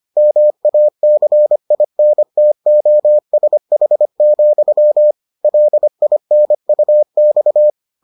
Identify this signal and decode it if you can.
Morse code, heard as a keyed tone, about 25 wpm, 600 Hz. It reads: MACINTOSH, LINUX